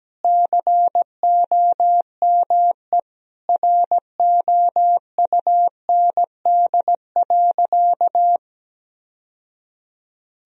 Morse code, heard as a keyed tone, about 17 wpm, 700 Hz. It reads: COME ROUND.
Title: Sentences from Top 1000 Words - Review Part 1 - 17wpm